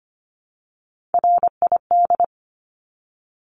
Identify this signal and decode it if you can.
Morse code, heard as a keyed tone, about 25 wpm, 700 Hz. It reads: RID
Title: Top 1751-2000 Words 25wpm